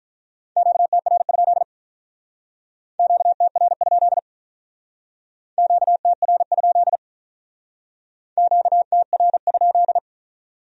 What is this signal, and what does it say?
Morse code, heard as a keyed tone, about 45 wpm, 700 Hz. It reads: QTR? QTR? QTR? QTR?